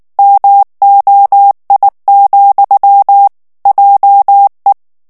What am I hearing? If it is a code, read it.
Morse code, heard as a keyed tone, about 19 wpm, 800 Hz. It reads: MOI, JE
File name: FRANCAIS TEXTE CLAIR 2 VITESSE 18 MOTS MINUTES